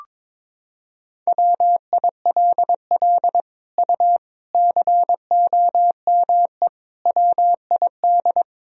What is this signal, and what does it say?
Morse code, heard as a keyed tone, about 22 wpm, 700 Hz. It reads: WILL U COME WID
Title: Sentences from Top 200 Words and Abbreviations 22wpm